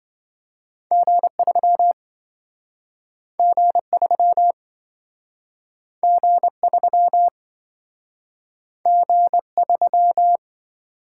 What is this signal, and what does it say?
Morse code, heard as a keyed tone, about 27 wpm, 700 Hz. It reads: G3 G3 G3 G3